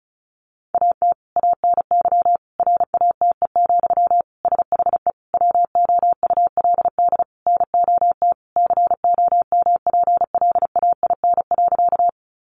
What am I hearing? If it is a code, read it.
Morse code, heard as a keyed tone, about 34 wpm, 700 Hz. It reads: AT ANY RATE, SHE WOULD NOT COMPLAIN.
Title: Sentences from Top 1300 Words 35wpm